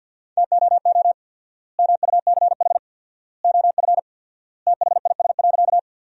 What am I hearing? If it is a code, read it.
Morse code, heard as a keyed tone, about 50 wpm, 700 Hz. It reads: TOO MUCH OF THIS.
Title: 4x Word-Spacing - Sentences from Top 200 Words - 50wpm